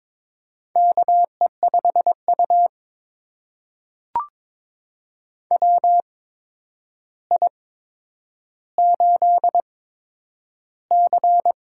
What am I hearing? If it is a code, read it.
Morse code, heard as a keyed tone, about 22 wpm, spaced as for 9 wpm, 700 Hz. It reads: KE5U WI8C
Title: USA Call Signs - 4 Characters - 8x Character Spacing - 22wpm